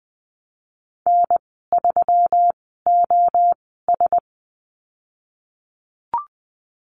Morse code, heard as a keyed tone, about 20 wpm, spaced as for 14 wpm, 700 Hz. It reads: N3OS
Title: 122 Lesson 20g - Number 3 - Call Signs 20wpm